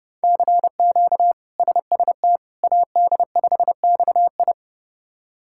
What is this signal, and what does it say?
Morse code, heard as a keyed tone, about 30 wpm, 700 Hz. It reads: CQ SST AD5XI
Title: K1USN SST QSO Practice - 30wpm